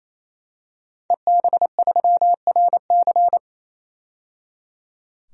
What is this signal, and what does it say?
Morse code, heard as a keyed tone, about 27 wpm, 700 Hz. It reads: EB3RC
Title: DXCC-100 Call Signs - 5 Characters - 28wpm